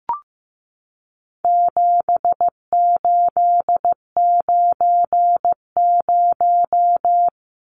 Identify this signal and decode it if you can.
Morse code, heard as a keyed tone, about 15 wpm, 700 Hz. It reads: 7890
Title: Numbers - 4-Digit 15wpm